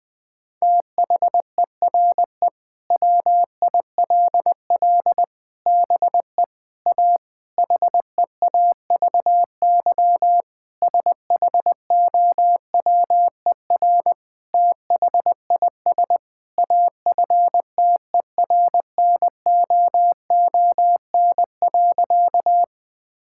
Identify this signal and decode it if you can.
Morse code, heard as a keyed tone, about 20 wpm, 700 Hz. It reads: THERE WILL BE A HEAVY SHOWER THIS AFTERNOON.
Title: Sentences from Top 1600 Words 20wpm